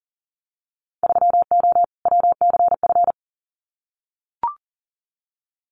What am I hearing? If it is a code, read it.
Morse code, heard as a keyed tone, about 40 wpm, 700 Hz. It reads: 3O WCF
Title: Winter Field Day Exchanges Encore 40wpm